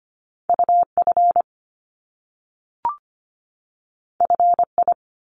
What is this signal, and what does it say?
Morse code, heard as a keyed tone, about 25 wpm, 700 Hz. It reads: U F F I
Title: Top Two Letter Combinations in Words 25wpm